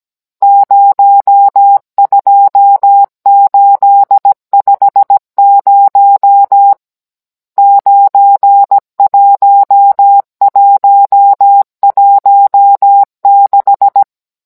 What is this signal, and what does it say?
Morse code, heard as a keyed tone, about 17 wpm, 800 Hz. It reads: 02850 91116
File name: Cours-NR18-16mm